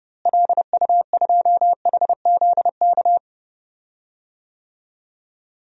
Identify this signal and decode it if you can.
Morse code, heard as a keyed tone, about 30 wpm, 700 Hz. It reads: LU2HZK
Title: DXCC-100 Call Signs - 6 Characters - Repeated 3x - 30wpm